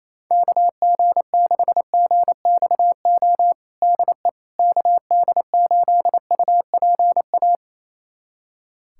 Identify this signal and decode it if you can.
Morse code, heard as a keyed tone, about 28 wpm, 700 Hz. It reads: KG6GXO DE KD8UPA
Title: USA Callsign DE Callsign - 6 Characters - 28wpm